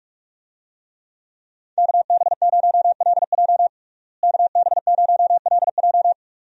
Morse code, heard as a keyed tone, about 46 wpm, 700 Hz. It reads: KB0LJ KB0LJ
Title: USA Call Signs - 5 Characters - Repeated 2x - 45wpm